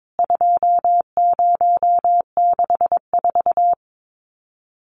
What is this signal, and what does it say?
Morse code, heard as a keyed tone, about 22 wpm, 700 Hz. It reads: 2064
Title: CWT Exchange 22wpm